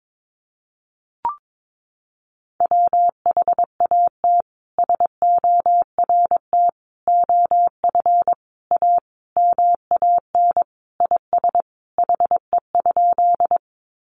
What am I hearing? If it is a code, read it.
Morse code, heard as a keyed tone, about 22 wpm, 700 Hz. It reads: WHAT SORT OF A MAN IS HE?
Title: Sentences from Top 1300 Words 22wpm